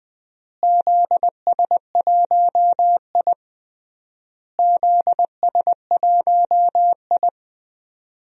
Morse code, heard as a keyed tone, about 20 wpm, 700 Hz. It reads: ZS1I ZS1I
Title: DXCC-100 Call Signs - 4 Characters - Repeated 3x - 20wpm